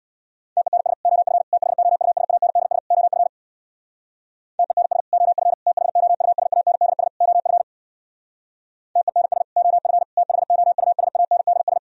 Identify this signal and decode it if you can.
Morse code, heard as a keyed tone, about 64 wpm, 700 Hz. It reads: TENS OF THOUSANDS OF TENS OF THOUSANDS OF TENS OF THOUSANDS